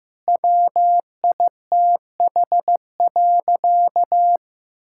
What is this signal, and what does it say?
Morse code, heard as a keyed tone, about 15 wpm, 700 Hz. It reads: WITH.
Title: Sentences from Top 1000 Words - Review Part 5 - 15wpm